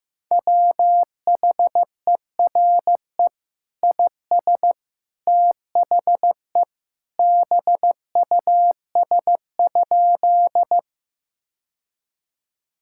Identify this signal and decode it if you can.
Morse code, heard as a keyed tone, about 15 wpm, 700 Hz. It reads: WHERE IS THE BUS?
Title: Sentences from Top 1000 Words - Review Part 6 - 15wpm